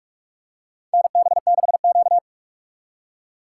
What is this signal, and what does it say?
Morse code, heard as a keyed tone, about 43 wpm, 700 Hz. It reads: NB6Q
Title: USA Call Signs - 2x Character Spacing - 45wpm